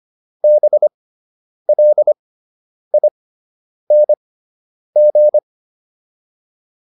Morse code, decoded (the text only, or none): BLING